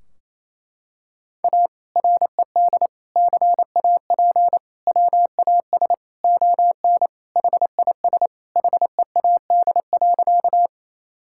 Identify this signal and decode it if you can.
Morse code, heard as a keyed tone, about 28 wpm, 700 Hz. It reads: A RED CAP WAS ON HIS HEAD.
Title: Sentences from Top 1600 Words 28wpm